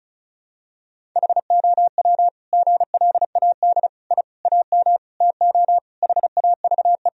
Morse code, heard as a keyed tone, about 36 wpm, 700 Hz. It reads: HOW GLAD I AM TO HAVE
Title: Sentences from Top 1000 Words - Review Part 2 - 35wpm